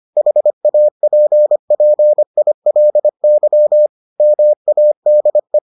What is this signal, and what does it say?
Morse code, heard as a keyed tone, about 25 wpm, 600 Hz. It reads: HAPPILY MADE